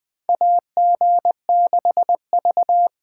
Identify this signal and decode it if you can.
Morse code, heard as a keyed tone, about 20 wpm, 700 Hz. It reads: AG6V
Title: 2023 SOTA Chaser Call Signs - 20wpm